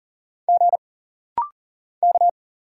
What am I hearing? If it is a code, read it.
Morse code, heard as a keyed tone, about 39 wpm, 700 Hz. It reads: G K
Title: Single Letter-Number Alt - Rapid-Fire 40wpm